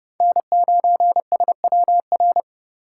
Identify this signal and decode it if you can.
Morse code, heard as a keyed tone, about 30 wpm, 700 Hz. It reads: N9SWR
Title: USA Call Signs - 5 Characters - 4x Character Spacing - 30wpm